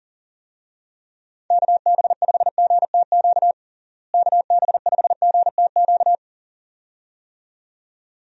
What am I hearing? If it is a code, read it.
Morse code, heard as a keyed tone, about 40 wpm, 700 Hz. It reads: KB5GTQ KB5GTQ
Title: USA Call Signs - 6 Characters - Repeated 2x - 40wpm